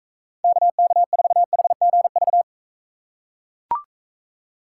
Morse code, heard as a keyed tone, about 42 wpm, 700 Hz. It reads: KK4HGV